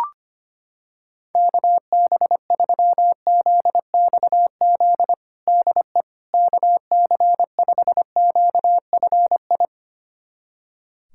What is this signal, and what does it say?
Morse code, heard as a keyed tone, about 25 wpm, 700 Hz. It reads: KB3ZXZ DE KC5QFI
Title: USA Callsign DE Callsign - 6 Characters - 25wpm